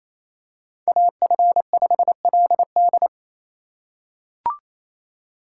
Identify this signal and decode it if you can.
Morse code, heard as a keyed tone, about 28 wpm, 700 Hz. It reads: AF5LD
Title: USA Call Signs - 5 Characters - 2x Character Spacing - 28wpm